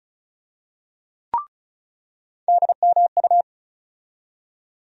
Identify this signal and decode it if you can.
Morse code, heard as a keyed tone, about 36 wpm, 700 Hz. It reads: DMU
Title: USA Call Sign Postfixes - 3 Letters 35wpm